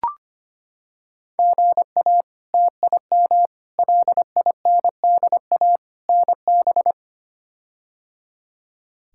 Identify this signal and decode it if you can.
Morse code, heard as a keyed tone, about 25 wpm, 700 Hz. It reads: GA TIM LINDA NB
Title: K1USN SST QSO Practice - 25wpm